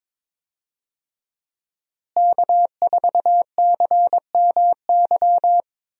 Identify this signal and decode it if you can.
Morse code, heard as a keyed tone, about 22 wpm, 700 Hz. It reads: K4CMY
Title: USA Call Signs - 5 Characters - Repeated 2x - 22wpm